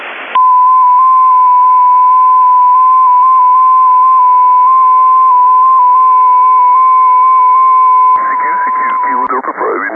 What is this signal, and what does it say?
FT8, with speech.